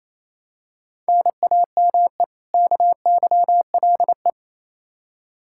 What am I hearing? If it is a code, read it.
Morse code, heard as a keyed tone, about 28 wpm, 700 Hz. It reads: NAME KYLE
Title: Practice QSO - Exchange 1 - Beginning - 28wpm